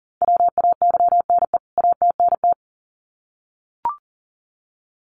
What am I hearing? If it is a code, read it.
Morse code, heard as a keyed tone, about 40 wpm, 700 Hz. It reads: WAYNE ATNT
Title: CWT Exchange 40wpm